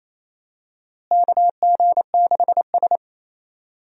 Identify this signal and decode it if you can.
Morse code, heard as a keyed tone, about 28 wpm, 700 Hz. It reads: KG6S
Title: USA Call Signs - 4 Characters - Repeated 3x - 28wpm